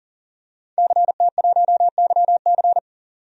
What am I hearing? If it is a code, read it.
Morse code, heard as a keyed tone, about 40 wpm, 700 Hz. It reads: CT1YC
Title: DXCC-100 Call Signs - 5 Characters - 40wpm